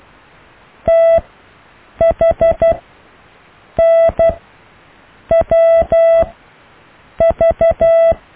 Morse code, heard as a keyed tone, about 12 wpm, spaced as for 6 wpm, 660 Hz. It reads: THNWV
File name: cw / EKTPV+BFHNW